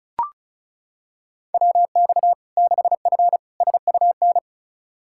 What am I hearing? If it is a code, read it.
Morse code, heard as a keyed tone, about 36 wpm, 700 Hz. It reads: WX 6F SUN
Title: Practice QSO - Exchange 2 - Info - 35wpm